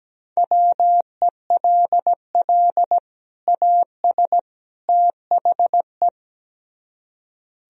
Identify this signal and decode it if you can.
Morse code, heard as a keyed tone, about 17 wpm, 700 Hz. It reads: WELL AS THE
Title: Speed-Racing - Sets of 3 Words 17wpm